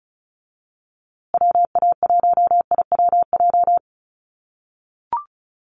Morse code, heard as a keyed tone, about 34 wpm, 700 Hz. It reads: WA1IWJ
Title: USA Call Signs Repeated 2x 35wpm